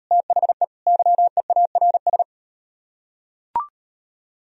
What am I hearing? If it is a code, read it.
Morse code, heard as a keyed tone, about 38 wpm, 700 Hz. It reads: THE YEARS